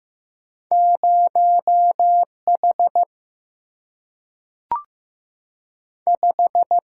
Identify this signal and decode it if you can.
Morse code, heard as a keyed tone, about 15 wpm, 700 Hz. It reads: 0H 5
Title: Single-Letter Single-Number 15wpm